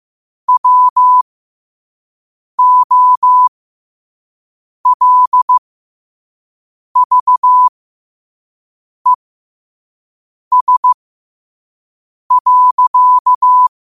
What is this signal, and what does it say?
Morse code, heard as a keyed tone, about 15 wpm, spaced as for 5 wpm, 1000 Hz. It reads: WOLVES.
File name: qotdfiles / 5wpm